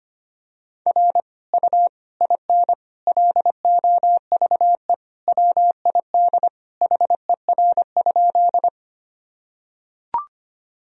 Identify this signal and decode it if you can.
Morse code, heard as a keyed tone, about 25 wpm, 700 Hz. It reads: R U IN LOVE WID HER?
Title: Sentences from Top 400 Words and Abbreviations 25wpm